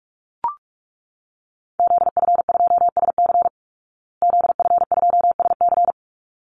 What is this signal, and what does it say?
Morse code, heard as a keyed tone, about 44 wpm, 700 Hz. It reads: ZF2SC ZF2SC
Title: Call Signs Repeated 3x - 45wpm